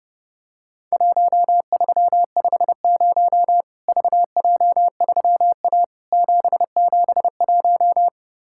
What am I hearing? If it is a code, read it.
Morse code, heard as a keyed tone, about 30 wpm, 700 Hz. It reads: 1350 VJ3A 771
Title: Worked All Europe DX Contest - QTC Exchange - 30wpm